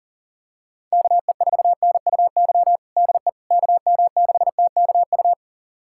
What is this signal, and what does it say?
Morse code, heard as a keyed tone, about 40 wpm, 700 Hz. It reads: KE4NUY DE KM6TKU
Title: USA Callsign DE Callsign 40wpm